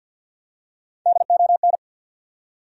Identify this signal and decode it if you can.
Morse code, heard as a keyed tone, about 53 wpm, 700 Hz. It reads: DON
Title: USA Common Names - Max 3 Letters - 50wpm